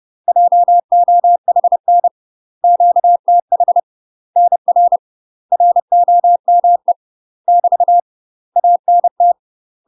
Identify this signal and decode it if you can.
Morse code, heard as a keyed tone, about 30 wpm, 700 Hz. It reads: JOHN QTH NR ROME <BT> ANT
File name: UK MASTER QSOs TRAINING / 30WPM / UK MASTER QSOs 0036